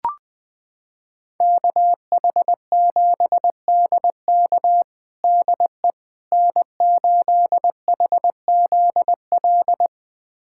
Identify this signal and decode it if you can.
Morse code, heard as a keyed tone, about 20 wpm, 700 Hz. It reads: KH7DK DE N8HZL
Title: USA Callsign DE Callsign - 5 Characters - 20wpm